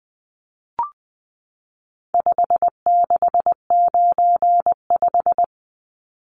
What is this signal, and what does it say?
Morse code, heard as a keyed tone, about 20 wpm, 700 Hz. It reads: 5695
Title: Numbers - 4-Digit 20wpm